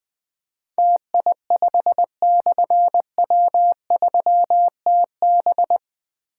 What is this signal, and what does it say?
Morse code, heard as a keyed tone, about 20 wpm, 700 Hz. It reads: TI5/W3TB
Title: Difficult Call Signs 20wpm